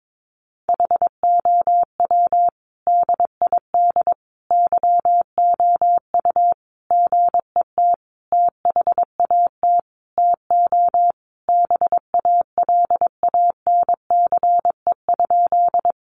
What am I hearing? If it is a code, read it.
Morse code, heard as a keyed tone, about 22 wpm, 700 Hz. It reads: HOW DID YOU GET THAT TO BALANCE?
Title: Sentences from Top 900 Words 22wpm